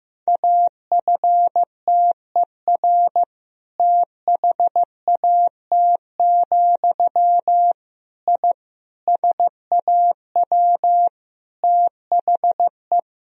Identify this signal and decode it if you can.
Morse code, heard as a keyed tone, about 16 wpm, 700 Hz. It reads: AFTER THAT, I SAW THE
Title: Sentences from Top 1000 Words - Review Part 1 - 15wpm